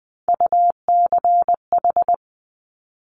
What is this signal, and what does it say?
Morse code, heard as a keyed tone, about 20 wpm, 700 Hz. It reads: UCH
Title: Top Three Letter Combinations in Words 20wpm